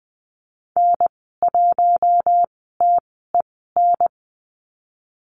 Morse code, heard as a keyed tone, about 19 wpm, spaced as for 14 wpm, 700 Hz. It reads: N1TEN
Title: 048 Lesson 9g - Number 4 - Call Signs 20wpm